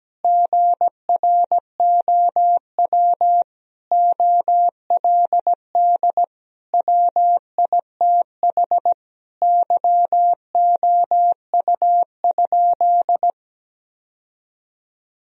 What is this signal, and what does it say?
Morse code, heard as a keyed tone, about 17 wpm, 700 Hz. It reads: GROW OLD WITH YOU?
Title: Sentences from Top 1000 Words - Review Part 1 - 17wpm